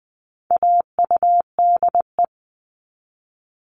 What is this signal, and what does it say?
Morse code, heard as a keyed tone, about 20 wpm, 700 Hz. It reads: AUDE
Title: All Cut Numbers - 4-Digit 20wpm